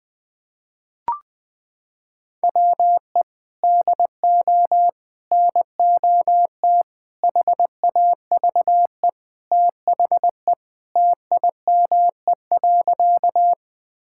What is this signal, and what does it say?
Morse code, heard as a keyed tone, about 19 wpm, 700 Hz. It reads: WE DO NOT HAVE THE TIME.
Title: Sentences with 5-Word Phrases - 20wpm